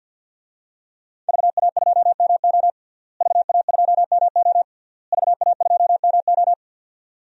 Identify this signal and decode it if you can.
Morse code, heard as a keyed tone, about 51 wpm, 700 Hz. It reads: VA2MO VA2MO VA2MO
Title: DXCC-100 Call Signs - 5 Characters - Repeated 3x - 50wpm